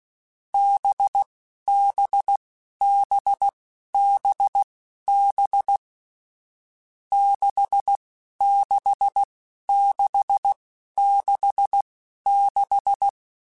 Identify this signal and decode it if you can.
Morse code, heard as a keyed tone, about 16 wpm, spaced as for 11 wpm, 800 Hz. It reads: BBBBB 66666
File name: Difficult Combos / Difficult Combos-mp3s / NDB NDB6 @10